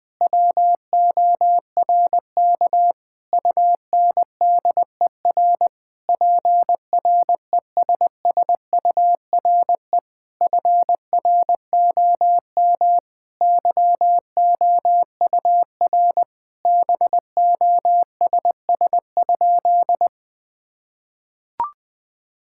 Morse code, heard as a keyed tone, about 21 wpm, 700 Hz. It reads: WORK UNDER PRESSURE FROM YOUR BOSS?
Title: Sentences from Top 1000 Words - Review Part 6 - 20wpm